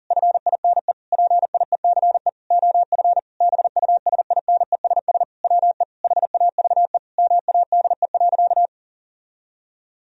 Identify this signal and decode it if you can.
Morse code, heard as a keyed tone, about 37 wpm, 700 Hz. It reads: FINE PIECE OF BUSINESS WE HAVE MADE.
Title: Sentences from Top 1000 Words - Review Part 2 - 40wpm